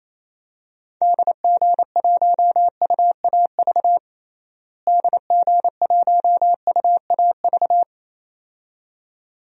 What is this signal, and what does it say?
Morse code, heard as a keyed tone, about 28 wpm, 700 Hz. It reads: DG1UAV DG1UAV
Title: DXCC-100 Call Signs - 6 Characters - Repeated 2x - 28wpm